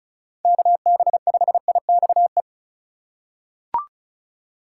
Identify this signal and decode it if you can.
Morse code, heard as a keyed tone, about 36 wpm, 700 Hz. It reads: KB5IXE
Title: USA Call Signs - 6 Characters - 4x Character Spacing - 35wpm